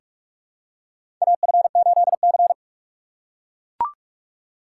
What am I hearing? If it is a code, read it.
Morse code, heard as a keyed tone, about 41 wpm, 690 Hz. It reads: AF8C
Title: USA Call Signs - 4 Characters - 8x Character Spacing - 45wpm